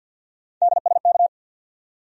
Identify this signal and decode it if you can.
Morse code, heard as a keyed tone, about 46 wpm, 700 Hz. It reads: DSK
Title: USA Call Sign Postfixes - 3 Letters 50wpm